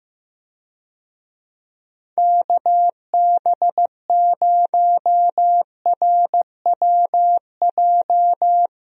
Morse code, heard as a keyed tone, about 15 wpm, 700 Hz. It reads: KB0RWJ
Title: USA Call Signs - 6 Characters - Repeated 2x - 15wpm